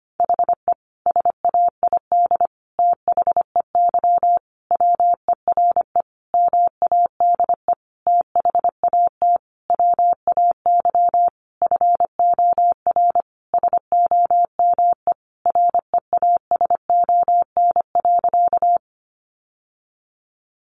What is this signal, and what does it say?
Morse code, heard as a keyed tone, about 25 wpm, 700 Hz. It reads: HE SAID THEY WERE MADE THAT WAY FOR SOME REASON.
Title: Sentences from Top 700 Words 25wpm